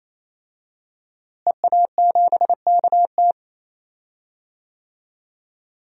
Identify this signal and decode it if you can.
Morse code, heard as a keyed tone, about 28 wpm, 700 Hz. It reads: EA7KT
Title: DXCC-100 Call Signs - 5 Characters - Repeated 3x - 28wpm